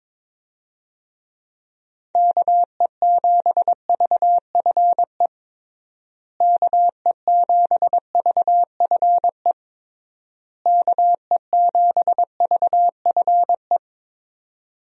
Morse code, heard as a keyed tone, about 22 wpm, 700 Hz. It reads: KE7VFE KE7VFE KE7VFE
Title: USA Call Signs - 6 Characters - Repeated 3x - 22wpm